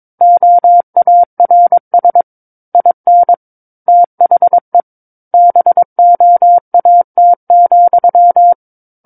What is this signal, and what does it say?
Morse code, heard as a keyed tone, about 22 wpm, 700 Hz. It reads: OARS IN THE BOAT,
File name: Book One / 13-Chapter 13 - 22 WPM